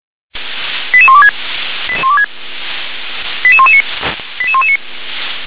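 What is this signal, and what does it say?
Five-tone selective call (ZVEI), heard as quick runs of five tones, tones of about 70 ms. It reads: 88126 88119 88119
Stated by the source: SDR-received ZVEI five-tone selective call, code 88634.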